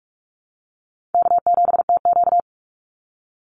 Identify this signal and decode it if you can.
Morse code, heard as a keyed tone, about 44 wpm, 700 Hz. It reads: K7TQ
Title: NA SOTA Chaser Callsigns 45wpm